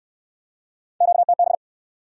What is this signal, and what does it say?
Morse code, heard as a keyed tone, about 64 wpm, 700 Hz. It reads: QTB